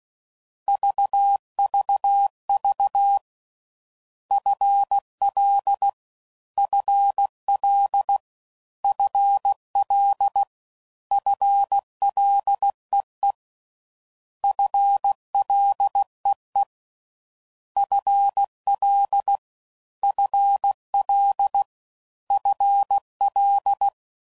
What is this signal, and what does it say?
Morse code, heard as a keyed tone, about 16 wpm, 800 Hz. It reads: VVV FL FL FL FLEE FLEE FL FL FL